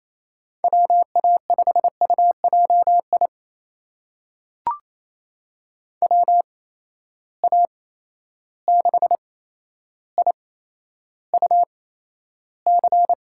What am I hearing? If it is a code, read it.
Morse code, heard as a keyed tone, about 28 wpm, spaced as for 12 wpm, 700 Hz. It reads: WA5UJI WA6IUC
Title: USA Call Signs - 6 Characters - 8x Character Spacing - 28wpm